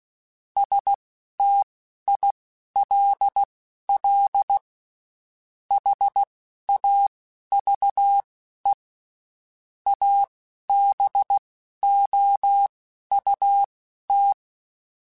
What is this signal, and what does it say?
Morse code, heard as a keyed tone, about 16 wpm, spaced as for 11 wpm, 800 Hz. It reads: STILL HAVE ABOUT
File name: (spaced) / QSOs Spaced-mp3s / WA7QOX 1 @15 sp